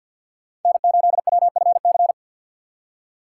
Morse code, heard as a keyed tone, about 52 wpm, 700 Hz. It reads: N8WFC
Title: USA Call Signs - 5 Characters - 6x Character Spacing - 50wpm